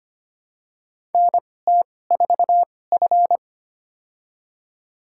Morse code, heard as a keyed tone, about 25 wpm, spaced as for 18 wpm, 700 Hz. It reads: NT4F
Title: USA Call Signs - 4 Characters - 2x Character Spacing - 25wpm